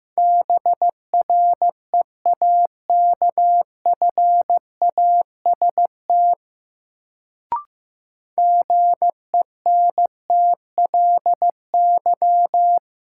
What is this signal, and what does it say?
Morse code, heard as a keyed tone, about 15 wpm, 700 Hz. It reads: BREAKFAST GENTLY
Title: Sentences from Top 1500 Words - New Word Drill - 15wpm